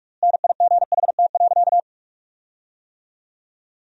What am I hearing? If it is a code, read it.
Morse code, heard as a keyed tone, about 46 wpm, 700 Hz. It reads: NIGHT.
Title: 4x Word-Spacing - Sentences from Top 500 Words - 45wpm